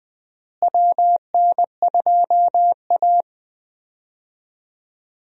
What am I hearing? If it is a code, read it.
Morse code, heard as a keyed tone, about 20 wpm, 700 Hz. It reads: WN2A
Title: USA Call Signs - 4 Characters - Repeated 2x - 20wpm